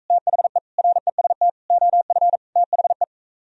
Morse code, heard as a keyed tone, about 46 wpm, 700 Hz. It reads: THE REST OF THE